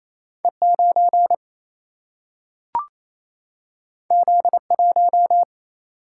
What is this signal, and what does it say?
Morse code, heard as a keyed tone, about 28 wpm, 700 Hz. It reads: E9 Z1
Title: Single-Letter Single-Number 28wpm